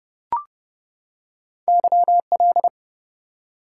Morse code, heard as a keyed tone, about 31 wpm, 700 Hz. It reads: YL
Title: Top 90 QSO Elements 30wpm